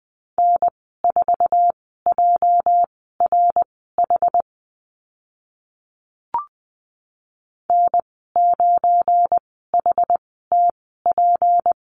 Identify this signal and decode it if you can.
Morse code, heard as a keyed tone, about 20 wpm, spaced as for 14 wpm, 700 Hz. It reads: N4JRH N9HTP
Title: 207 Lesson 33g - Letter j - Call Signs 20wpm